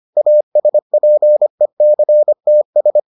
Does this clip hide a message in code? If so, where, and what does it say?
Yes: Morse code from 0.2 to 3.0 s, heard as a keyed tone, about 25 wpm, 600 Hz. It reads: ASPECTS